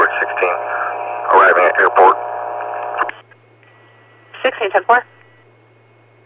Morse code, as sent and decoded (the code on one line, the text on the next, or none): none